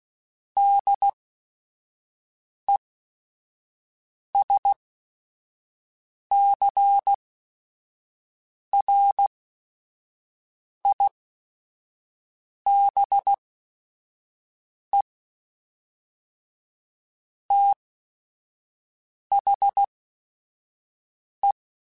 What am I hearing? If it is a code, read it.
Morse code, heard as a keyed tone, about 16 wpm, spaced as for 5 wpm, 800 Hz. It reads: DESCRIBE THE
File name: Technical Subjects / Technical Subjects-mp3s / Reactance @5